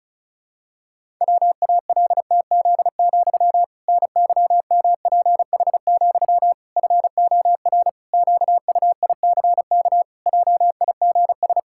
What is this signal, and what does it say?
Morse code, heard as a keyed tone, about 36 wpm, 700 Hz. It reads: WALTZ, NYMPH, FOR QUICK JIGS
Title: Pangram Challenge v2 - 35wpm